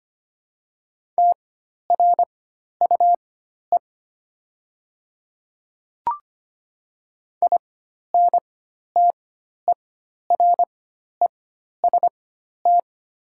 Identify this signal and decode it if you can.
Morse code, heard as a keyed tone, about 25 wpm, spaced as for 12 wpm, 700 Hz. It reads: TRUE INTEREST
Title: 4x Character Spacing - Top 500 Words - 25wpm